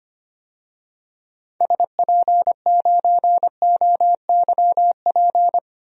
Morse code, heard as a keyed tone, about 25 wpm, 700 Hz. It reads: SP9OYP
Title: DXCC-100 Call Signs - 6 Characters - 25wpm